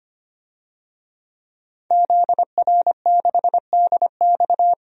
Morse code, heard as a keyed tone, about 25 wpm, 700 Hz. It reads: ZR6DX
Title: DXCC-100 Call Signs - 5 Characters - 25wpm